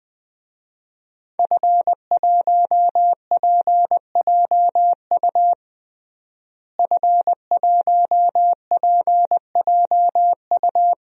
Morse code, heard as a keyed tone, about 20 wpm, 700 Hz. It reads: F1PJU F1PJU
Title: DXCC-100 Call Signs - 5 Characters - Repeated 2x - 20wpm